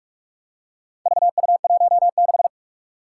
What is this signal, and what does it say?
Morse code, heard as a keyed tone, about 41 wpm, 700 Hz. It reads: VU16
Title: Four Challenging Letter-Number Combinations 45wpm